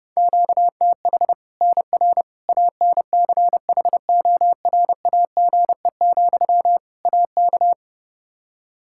Morse code, heard as a keyed tone, about 32 wpm, 700 Hz. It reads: QTH NR ANCHORAGE, AK
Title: K1USN SST QSO Practice 30wpm